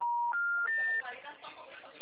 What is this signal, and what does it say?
Special information tone, with speech.